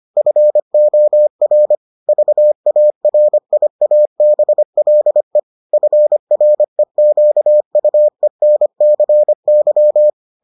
Morse code, heard as a keyed tone, about 25 wpm, 600 Hz. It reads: FOR VARIABLE FREQUENCY